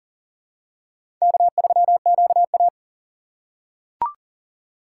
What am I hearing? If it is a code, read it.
Morse code, heard as a keyed tone, about 40 wpm, 700 Hz. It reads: K3QA
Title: USA Call Signs - 4 Characters - 4x Character Spacing - 40wpm